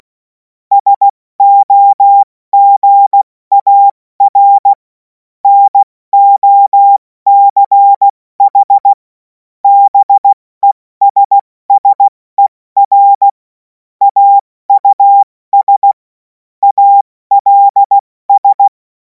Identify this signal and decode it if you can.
Morse code, heard as a keyed tone, about 16 wpm, 800 Hz. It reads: SOGAR NOCH BESSER AUS ALS
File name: Morsekurs / Witze / Witz 11. August 2026 70 F80